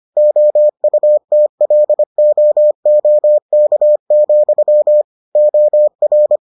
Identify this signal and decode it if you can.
Morse code, heard as a keyed tone, about 25 wpm, 600 Hz. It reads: OUTLOOK, OR